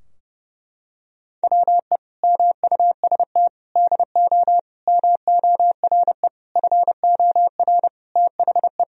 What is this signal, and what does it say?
Morse code, heard as a keyed tone, about 32 wpm, 700 Hz. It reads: WE MUST DO MORE FOR THE
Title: Sentences from Top 1000 Words - Review Part 3 - 30wpm